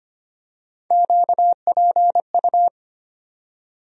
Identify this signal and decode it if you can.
Morse code, heard as a keyed tone, about 25 wpm, 700 Hz. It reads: QPU
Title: USA Call Sign Postfixes - 3 Letters 25wpm